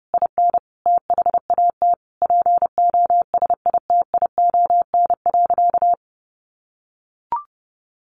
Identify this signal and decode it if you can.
Morse code, heard as a keyed tone, about 30 wpm, 700 Hz. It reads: IN THAT POSITION.
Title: Sentences from Top 900 Words 30wpm